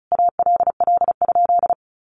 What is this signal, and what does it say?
Morse code, heard as a keyed tone, about 34 wpm, 700 Hz. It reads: ALL?
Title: Sentences from Top 1400 Words 35wpm